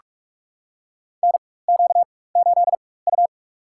Morse code, heard as a keyed tone, about 46 wpm, spaced as for 21 wpm, 700 Hz. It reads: NQ8U
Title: USA Call Signs - 4x Character Spacing - 45wpm